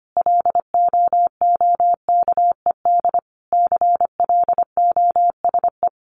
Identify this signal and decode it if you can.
Morse code, heard as a keyed tone, about 25 wpm, 700 Hz. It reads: LOOKED CLOSE
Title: A Sound Of Thunder 25wpm